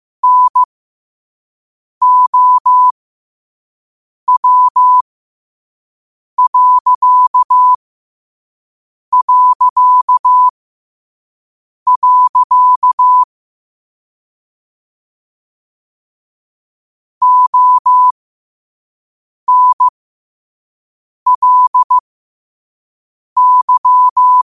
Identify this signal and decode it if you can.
Morse code, heard as a keyed tone, about 15 wpm, spaced as for 5 wpm, 1000 Hz. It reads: NOW... ONLY